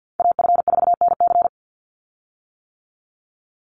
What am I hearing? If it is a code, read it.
Morse code, heard as a keyed tone, about 50 wpm, 700 Hz. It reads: AF4NC
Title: Call Signs - 50wpm